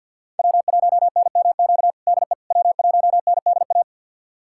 Morse code, heard as a keyed tone, about 50 wpm, 700 Hz. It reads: W1NMQ DE W1NDA
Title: USA Callsign DE Callsign 50wpm